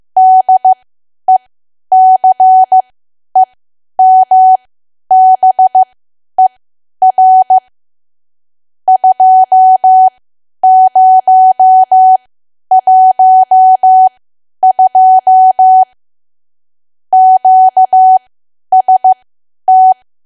Morse code, heard as a keyed tone, about 15 wpm, spaced as for 10 wpm, 750 Hz. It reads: DECEMBER 2012 QST
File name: cwa / W1AW-Level2 / 10WPM